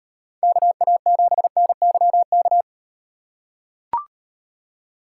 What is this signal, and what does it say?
Morse code, heard as a keyed tone, about 35 wpm, 700 Hz. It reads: KA7NYK